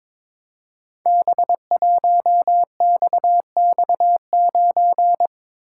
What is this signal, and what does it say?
Morse code, heard as a keyed tone, about 22 wpm, 700 Hz. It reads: B1XX9
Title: Code Groups - 5 Letter:Numbers - 22wpm